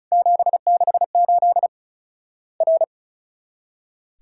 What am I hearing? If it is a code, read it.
Morse code, heard as a keyed tone, about 36 wpm, 700 Hz. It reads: 768 R
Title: Worked All Europe DX Contest - QTC Exchange - 35wpm